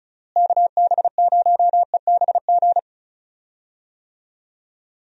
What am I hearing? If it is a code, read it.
Morse code, heard as a keyed tone, about 36 wpm, 700 Hz. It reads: KB0EBG
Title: USA Call Signs - 6 Characters - Repeated 2x - 35wpm